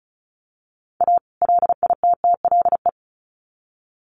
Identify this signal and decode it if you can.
Morse code, heard as a keyed tone, about 34 wpm, 700 Hz. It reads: A LITTLE
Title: Sets of 2 Words 35wpm